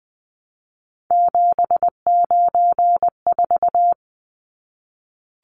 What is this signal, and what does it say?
Morse code, heard as a keyed tone, about 20 wpm, 700 Hz. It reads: 794
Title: Numbers - 3-Digit 20wpm